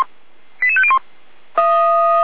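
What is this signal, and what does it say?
Five-tone selective call (ZVEI), heard as quick runs of five tones, tones of about 70 ms. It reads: 88501
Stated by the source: ZVEI five-tone selective call received by SDR.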